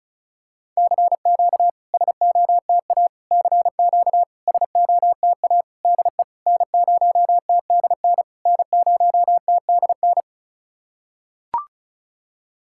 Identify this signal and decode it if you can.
Morse code, heard as a keyed tone, about 36 wpm, 700 Hz. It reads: CQ SOTA CQ SOTA DE N0TDN N0TDN
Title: SOTA QSO Practice 35wpm